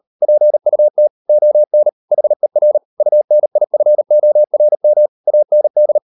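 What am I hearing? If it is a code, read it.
Morse code, heard as a keyed tone, about 38 wpm, 600 Hz. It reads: PUT ON HER UNIFORM AND
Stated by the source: Morse code practice at 38 WPM, sending Ken Follett's novel Never in English, from CW med Gnister.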